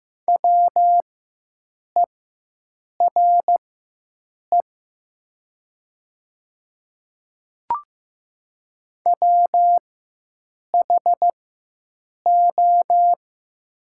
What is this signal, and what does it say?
Morse code, heard as a keyed tone, about 15 wpm, spaced as for 7 wpm, 700 Hz. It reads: WERE WHO
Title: 4x Character Spacing - Top 100 Words - 15wpm